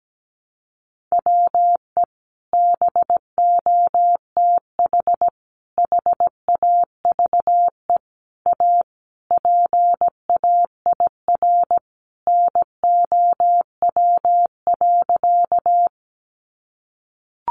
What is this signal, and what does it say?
Morse code, heard as a keyed tone, about 17 wpm, 700 Hz. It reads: WE BOTH HAVE A PAIR NOW.
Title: Sentences from Top 1200 Words 17wpm